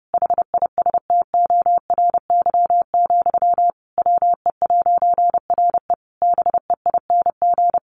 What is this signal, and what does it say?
Morse code, heard as a keyed tone, about 30 wpm, 700 Hz. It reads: HISTORY, WE'RE BEING
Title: A Sound Of Thunder 30wpm